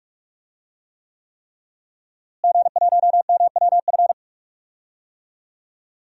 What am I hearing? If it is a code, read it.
Morse code, heard as a keyed tone, about 46 wpm, 700 Hz. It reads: G1MWF
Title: DXCC-100 Call Signs - 5 Characters - Repeated 3x - 45wpm